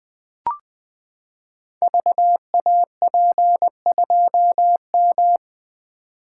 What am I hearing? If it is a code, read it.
Morse code, heard as a keyed tone, about 20 wpm, 700 Hz. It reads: VAP2M
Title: Code Groups - 5 Letter:Numbers - 20wpm